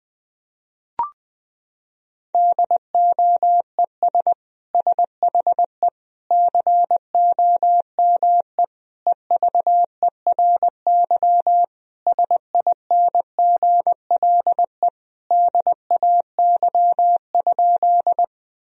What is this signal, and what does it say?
Morse code, heard as a keyed tone, about 20 wpm, 700 Hz. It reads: DOES SHE COME EVERY SINGLE DAY?
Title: Sentences from Top 1000 Words - Review Part 4 - 20wpm